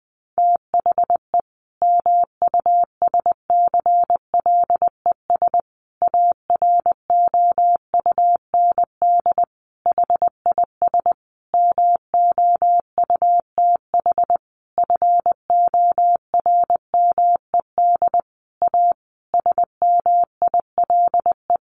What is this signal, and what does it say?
Morse code, heard as a keyed tone, about 20 wpm, 700 Hz. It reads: THE MUSCLES AROUND HIS MOUTH FORMED A SMILE
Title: A Sound Of Thunder 20wpm